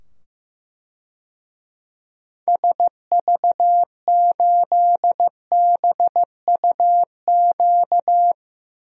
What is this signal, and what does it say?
Morse code, heard as a keyed tone, about 15 wpm, 700 Hz. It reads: SV8BUQ
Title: DXCC-100 Call Signs - 6 Characters - 15wpm